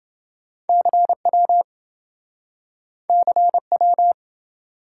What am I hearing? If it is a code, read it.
Morse code, heard as a keyed tone, about 26 wpm, 700 Hz. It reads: CW CW